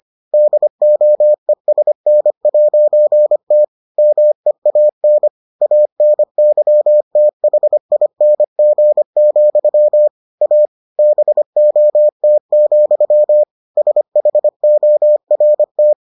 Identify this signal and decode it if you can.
Morse code, heard as a keyed tone, about 25 wpm, 600 Hz. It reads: DOESN'T MEAN ANYTHING, A BOT, SHORT